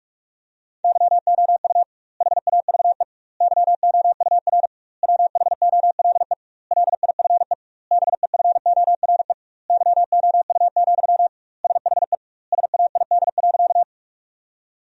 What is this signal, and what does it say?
Morse code, heard as a keyed tone, about 41 wpm, 700 Hz. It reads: YOU HAVE YOUR WHOLE LIFE BEFORE YOU, SHE SAID.
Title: 2x Word-Spacing - Sentences from Top 400 Words - 45wpm